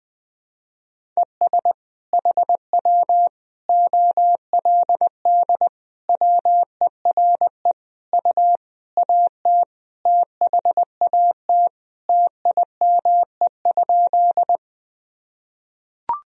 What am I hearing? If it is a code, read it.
Morse code, heard as a keyed tone, about 21 wpm, 700 Hz. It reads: ES HW OLD WERE U AT THAT TIME?
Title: Sentences from Top 200 Words and Abbreviations 20wpm